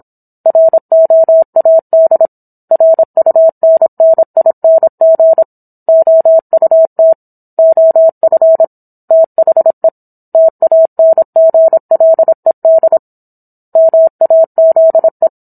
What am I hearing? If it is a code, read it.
Morse code, heard as a keyed tone, about 27 wpm, 650 Hz. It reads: ROAD RUNNING OUT OF THE TANGLED MAZE